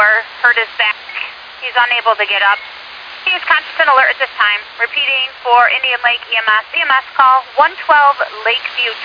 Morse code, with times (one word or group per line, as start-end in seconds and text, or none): none